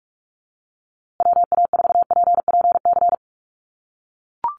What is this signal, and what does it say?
Morse code, heard as a keyed tone, about 44 wpm, 700 Hz. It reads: WA4PPC
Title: USA Call Signs Repeated 2x 45wpm